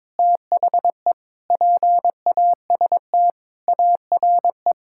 Morse code, heard as a keyed tone, about 22 wpm, 700 Hz. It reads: THE PAST ARE
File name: practice / 22wpm / Sets of 8 Words - 22wpm